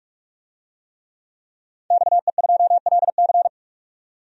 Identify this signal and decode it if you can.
Morse code, heard as a keyed tone, about 46 wpm, 700 Hz. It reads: XE2LC
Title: DXCC-100 Call Signs - 5 Characters - 45wpm